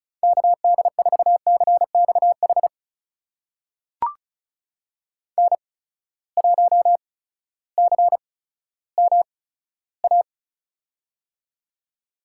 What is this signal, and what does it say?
Morse code, heard as a keyed tone, about 36 wpm, spaced as for 16 wpm, 700 Hz. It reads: KD4CXH N1CMA
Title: USA Call Signs - 8x Character Spacing - 35wpm